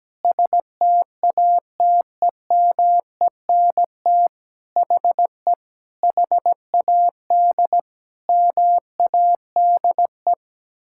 Morse code, heard as a keyed tone, about 17 wpm, 700 Hz. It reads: STATEMENT HE HAD MADE
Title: Sentences from Top 1600 Words 17wpm